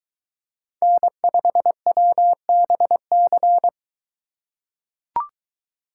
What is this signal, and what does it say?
Morse code, heard as a keyed tone, about 21 wpm, 700 Hz. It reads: N5WBC